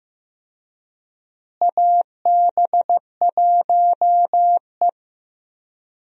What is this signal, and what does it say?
Morse code, heard as a keyed tone, about 15 wpm, 700 Hz. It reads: AB1E
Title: USA Call Signs - 4 Characters - Repeated 3x - 15wpm